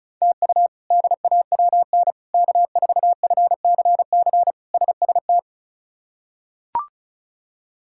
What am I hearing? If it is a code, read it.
Morse code, heard as a keyed tone, about 36 wpm, 700 Hz. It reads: TU DAWN K4FCC SST
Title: K1USN SST QSO Practice - 35wpm